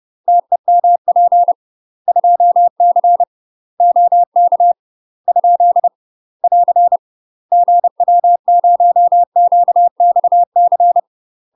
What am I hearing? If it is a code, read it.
Morse code, heard as a keyed tone, about 30 wpm, 700 Hz. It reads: TEMP 2C OK ? <AR> GW0QXC
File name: UK MASTER QSOs TRAINING / 30WPM / UK MASTER QSOs 0014